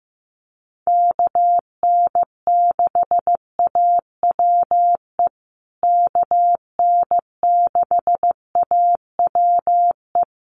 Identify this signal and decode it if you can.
Morse code, heard as a keyed tone, about 15 wpm, 700 Hz. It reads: KN6AWE KN6AWE
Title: USA Call Signs Repeated 3x 15wpm